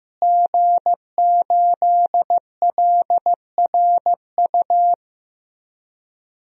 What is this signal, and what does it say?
Morse code, heard as a keyed tone, about 15 wpm, 700 Hz. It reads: G8LRU
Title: DXCC-100 Call Signs - 5 Characters - Repeated 3x - 15wpm